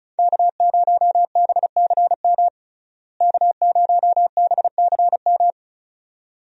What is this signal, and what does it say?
Morse code, heard as a keyed tone, about 36 wpm, 700 Hz. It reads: K0BCM K0BCM
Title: USA Call Signs - 5 Characters - Repeated 2x - 35wpm